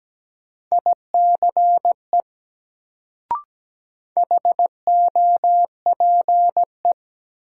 Morse code, heard as a keyed tone, about 17 wpm, 700 Hz. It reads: ICE HOPE
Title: Sentences from Top 1000 Words - New Word Drill - 17wpm